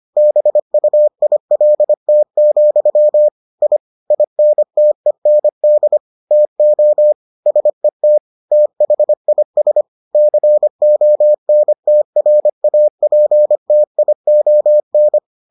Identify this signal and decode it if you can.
Morse code, heard as a keyed tone, about 25 wpm, 600 Hz. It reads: BUILT, I INTEND TO SET THIS CONTRAPTION